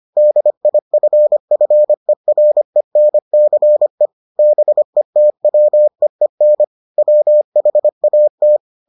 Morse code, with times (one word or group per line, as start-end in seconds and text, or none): 0.2-4.1 DIFFERENCE
4.4-6.6 BETWEEN
7.0-8.6 WHAT